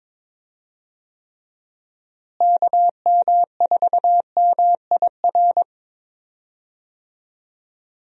Morse code, heard as a keyed tone, about 22 wpm, 700 Hz. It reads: KM4MIR
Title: USA Call Signs - 6 Characters - Repeated 2x - 22wpm